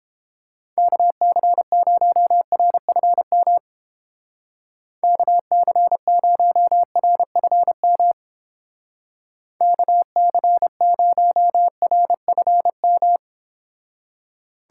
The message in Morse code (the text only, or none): KC0RFM KC0RFM KC0RFM